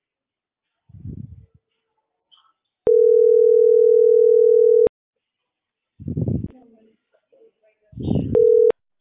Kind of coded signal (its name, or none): telephone call progress tone